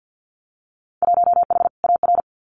Morse code, heard as a keyed tone, about 50 wpm, 700 Hz. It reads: 1H AR
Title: Winter Field Day Exchanges 50wpm